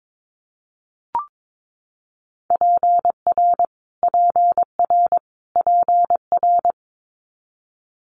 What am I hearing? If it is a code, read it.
Morse code, heard as a keyed tone, about 22 wpm, 700 Hz. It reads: PR PR PR
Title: Multi-letter Phonemes Repeated 3x 22wpm